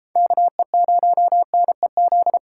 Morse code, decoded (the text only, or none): KE0NEZ